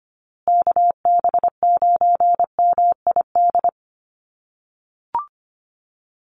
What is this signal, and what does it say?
Morse code, heard as a keyed tone, about 25 wpm, 700 Hz. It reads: KB9MID
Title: USA Call Signs Repeated 2x 25wpm